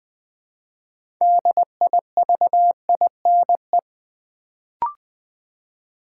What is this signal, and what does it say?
Morse code, heard as a keyed tone, about 20 wpm, 700 Hz. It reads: DIVINE
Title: Build A Word - 3 Letters At A Time - 6-Letter Words - 20wpm